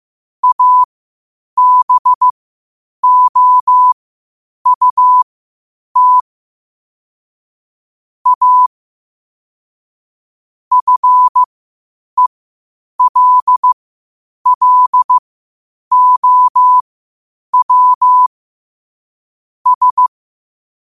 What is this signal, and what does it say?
Morse code, heard as a keyed tone, about 15 wpm, spaced as for 8 wpm, 1000 Hz. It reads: ABOUT A FELLOW S
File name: qotdfiles / 8wpm